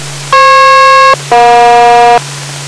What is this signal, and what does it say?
Aircraft SELCAL, heard as two pairs of tones. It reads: FP-EJ